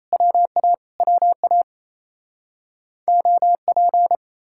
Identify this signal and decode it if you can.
Morse code, heard as a keyed tone, about 38 wpm, 700 Hz. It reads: WA WA OP